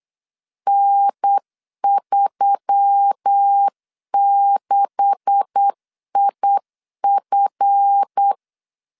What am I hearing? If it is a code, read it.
Morse code, heard as a keyed tone, about 8 wpm, 800 Hz. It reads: N36IF
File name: LEZIONI / 9 286 8wpm 3425